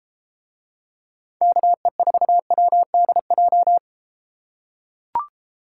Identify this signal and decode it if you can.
Morse code, heard as a keyed tone, about 31 wpm, 700 Hz. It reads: KE4WDJ